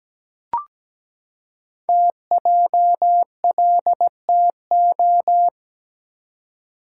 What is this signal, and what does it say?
Morse code, heard as a keyed tone, about 17 wpm, 700 Hz. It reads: TJLTO
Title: Code Groups - 5 Letters - 17wpm